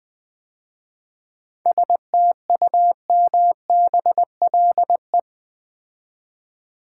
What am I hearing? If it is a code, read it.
Morse code, heard as a keyed tone, about 20 wpm, 700 Hz. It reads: STUMBLE
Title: 7 Letter Words - First 3 Letters Given - 20wpm